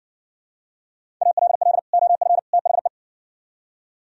Morse code, heard as a keyed tone, about 69 wpm, 700 Hz. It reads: ALL OF THE